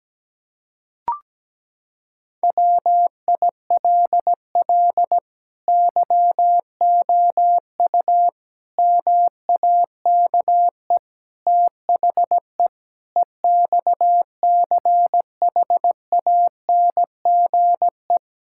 Morse code, heard as a keyed tone, about 17 wpm, 700 Hz. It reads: WILL YOU MAKE THE EXCHANGE
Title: Sentences from Top 1000 Words - Review Part 6 - 17wpm